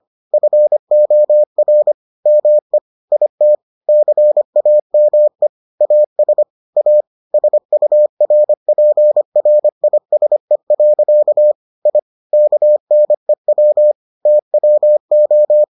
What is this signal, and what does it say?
Morse code, heard as a keyed tone, about 25 wpm, 600 Hz. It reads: FOR ME IT CAME AS A SURPRISE. I KNEW TWO